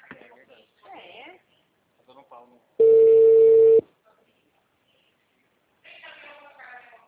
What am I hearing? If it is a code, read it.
Ringback tone (China, 450 Hz), with speech.